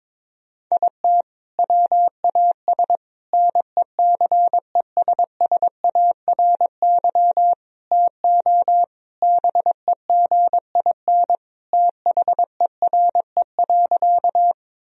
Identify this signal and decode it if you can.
Morse code, heard as a keyed tone, about 22 wpm, 700 Hz. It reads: IT WAS NECESSARY TO BEGIN THERE.
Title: Sentences from Top 1000 Words - Review Part 1 - 22wpm